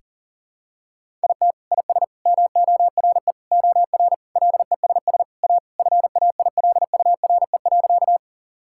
Morse code, heard as a keyed tone, about 40 wpm, 700 Hz. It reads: IT IS MORE OR LESS A FAILURE.
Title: Sentences from Top 1000 Words - Review Part 6 - 40wpm